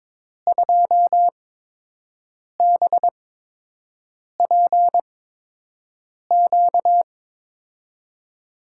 Morse code, heard as a keyed tone, about 22 wpm, spaced as for 6 wpm, 700 Hz. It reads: 2BPQ
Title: USA Call Signs - 5 Characters - 8x Character Spacing - 22wpm